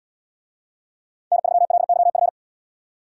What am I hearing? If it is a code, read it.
Morse code, heard as a keyed tone, about 73 wpm, 700 Hz. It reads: N3LYF